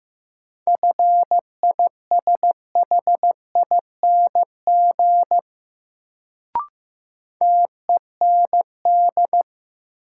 Morse code, heard as a keyed tone, about 15 wpm, 700 Hz. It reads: FISHING TEND
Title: Sentences from Top 1000 Words - Review Part 5 - New Word Drill - 15wpm